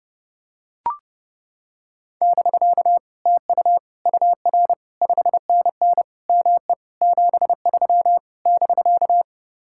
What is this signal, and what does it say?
Morse code, heard as a keyed tone, about 29 wpm, 700 Hz. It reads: <BK> TU UR 5NN ME 73 <BK>